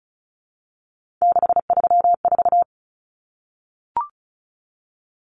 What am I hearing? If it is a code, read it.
Morse code, heard as a keyed tone, about 34 wpm, 700 Hz. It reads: 634
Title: Common Cut Numbers - 3-Digit 35wpm